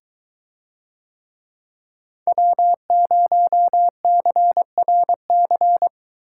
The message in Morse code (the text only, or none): W0CRC